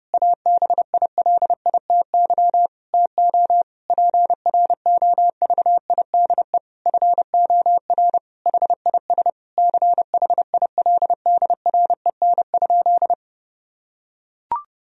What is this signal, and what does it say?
Morse code, heard as a keyed tone, about 30 wpm, 700 Hz. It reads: ABILITY TO PROVIDE FOR HIS CHILDREN?
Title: Sentences from Top 1000 Words - Review Part 3 - 30wpm